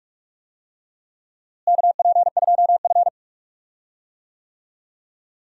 Morse code, heard as a keyed tone, about 42 wpm, 700 Hz. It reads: KP2F
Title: DXCC-100 Call Signs - 4 Characters - Repeated 2x - 45wpm